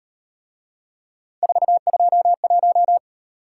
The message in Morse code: 421

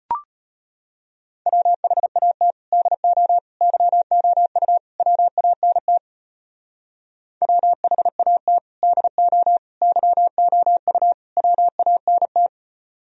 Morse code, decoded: WHAT DO YOU WANT WHAT DO YOU WANT

